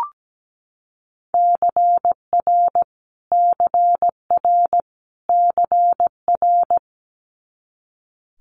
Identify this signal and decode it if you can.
Morse code, heard as a keyed tone, about 17 wpm, 700 Hz. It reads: CR CR CR
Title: Multi-letter Phonemes Repeated 3x 17wpm